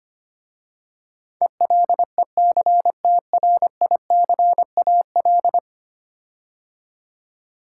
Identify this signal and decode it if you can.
Morse code, heard as a keyed tone, about 25 wpm, 700 Hz. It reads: ELECTRICAL
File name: 10 Letter Words - First 3 Letters Given - 25wpm